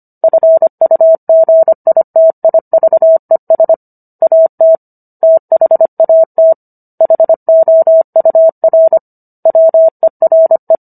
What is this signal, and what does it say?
Morse code, heard as a keyed tone, about 25 wpm, 650 Hz. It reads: FUGITIVES AT THAT HOUR WERE